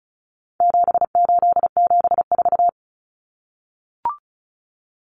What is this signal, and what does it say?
Morse code, heard as a keyed tone, about 34 wpm, 700 Hz. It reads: 7874
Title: Common Cut Numbers - 4-Digit 35wpm